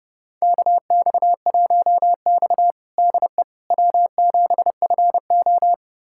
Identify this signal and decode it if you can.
Morse code, heard as a keyed tone, about 30 wpm, 700 Hz. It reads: KX1X DE W7FO
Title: USA Callsign DE Callsign - 4 Characters - 30wpm